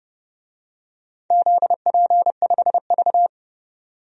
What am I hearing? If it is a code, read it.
Morse code, heard as a keyed tone, about 30 wpm, 700 Hz. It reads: ZP5V
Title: DXCC-100 Call Signs - 4 Characters - 30wpm